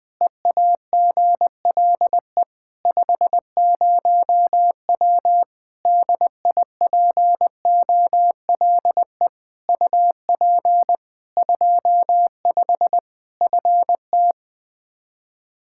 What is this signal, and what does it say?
Morse code, heard as a keyed tone, about 20 wpm, 700 Hz. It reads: EAGLE 50W DIPOLE UP 25 FT
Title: Practice QSO - Exchange 2 - Info - 20wpm